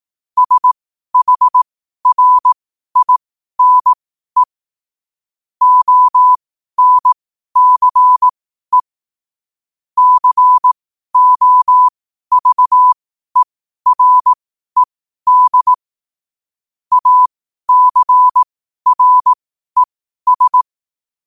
Morse code, heard as a keyed tone, about 18 wpm, spaced as for 12 wpm, 1000 Hz. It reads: SHRINE ONCE COVERED ACRES
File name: qotdfiles / 12wpm